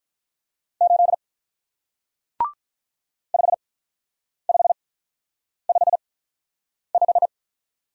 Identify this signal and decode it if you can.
Morse code, heard as a keyed tone, about 48 wpm, spaced as for 9 wpm, 700 Hz. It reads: 8 5555